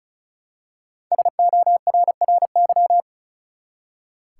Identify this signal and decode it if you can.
Morse code, heard as a keyed tone, about 36 wpm, 700 Hz. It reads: SORRY
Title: Five Letter Words - 35wpm